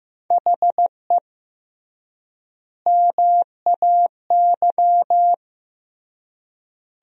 Morse code, heard as a keyed tone, about 16 wpm, 700 Hz. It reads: HE MAY
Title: 3x Word-Spacing - Sentences from Top 300 Words - 15wpm